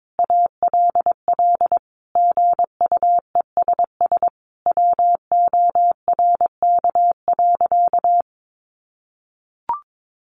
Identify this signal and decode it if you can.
Morse code, heard as a keyed tone, about 22 wpm, 700 Hz. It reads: ALL GUESS WORK.
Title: Sentences from Top 1500 Words 22wpm